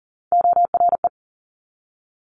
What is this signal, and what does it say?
Morse code, heard as a keyed tone, about 38 wpm, 700 Hz. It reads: ORE
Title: Multi-letter Phonemes 40wpm